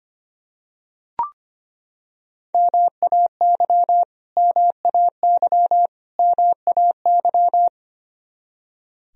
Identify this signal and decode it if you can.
Morse code, heard as a keyed tone, about 25 wpm, 700 Hz. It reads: MAY MAY MAY
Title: Top 100 Words Repeated 25wpm